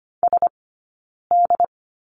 Morse code, heard as a keyed tone, about 25 wpm, spaced as for 9 wpm, 690 Hz. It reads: SD
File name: course / 25:10wpm / 092 Lesson 16d - Letter c - Two Characters 25wpm